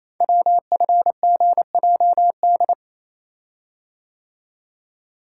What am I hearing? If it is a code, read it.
Morse code, heard as a keyed tone, about 28 wpm, 690 Hz. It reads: WFGJD
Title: Code Groups - 5 Letter:Numbers - 28wpm